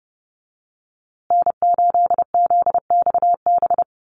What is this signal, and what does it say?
Morse code, heard as a keyed tone, about 30 wpm, 700 Hz. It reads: N8ZXB
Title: USA Call Signs Repeated 3x 30wpm